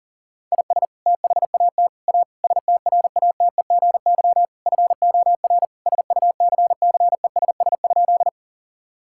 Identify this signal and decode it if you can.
Morse code, heard as a keyed tone, about 40 wpm, 700 Hz. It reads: IS THAT A STRATEGY FOR SUCCESS?
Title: Sentences from Top 1000 Words - Review Part 5 - 40wpm